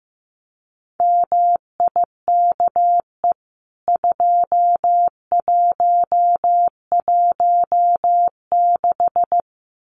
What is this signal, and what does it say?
Morse code, heard as a keyed tone, about 15 wpm, 700 Hz. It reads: MIKE 2116
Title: CWT Exchange 15wpm